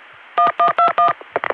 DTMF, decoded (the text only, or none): 1121